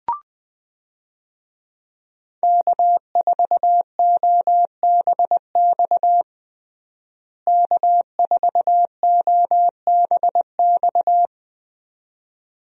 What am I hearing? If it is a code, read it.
Morse code, heard as a keyed tone, about 20 wpm, 700 Hz. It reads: K4OBX K4OBX
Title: USA Call Signs - 5 Characters - Repeated 2x - 20wpm